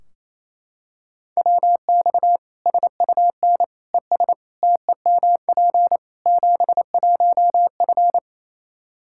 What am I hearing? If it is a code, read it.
Morse code, heard as a keyed tone, about 27 wpm, 700 Hz. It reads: WX SUN ES TEMP 71F
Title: Practice QSO - Exchange 2 - Info - 28wpm